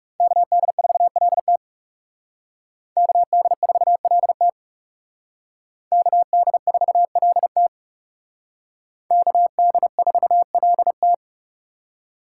Morse code, heard as a keyed tone, about 38 wpm, 700 Hz. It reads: KD4LT KD4LT KD4LT KD4LT